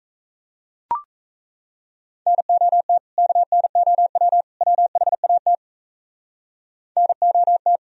Morse code, heard as a keyed tone, about 34 wpm, 700 Hz. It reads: NOT KNOW WHAT NOT
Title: Speed-Racing - Sets of 3 Words 28wpm